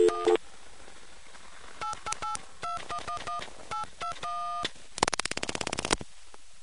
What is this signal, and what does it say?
DTMF, heard as telephone touch tones, tones of about 130 ms. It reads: 1 8882111821